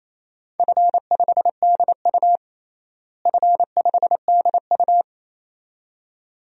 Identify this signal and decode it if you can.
Morse code, heard as a keyed tone, about 28 wpm, 700 Hz. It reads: F5DU F5DU
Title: DXCC-100 Call Signs - 4 Characters - Repeated 2x - 28wpm